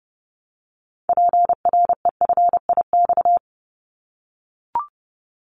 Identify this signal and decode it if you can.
Morse code, heard as a keyed tone, about 30 wpm, 700 Hz. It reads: PREFIX
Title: Callsign Prefixes for Top 50 DXCC - 30wpm